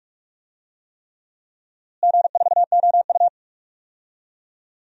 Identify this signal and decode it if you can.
Morse code, heard as a keyed tone, about 46 wpm, 700 Hz. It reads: G4OU
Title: DXCC-100 Call Signs - 4 Characters - Repeated 3x - 45wpm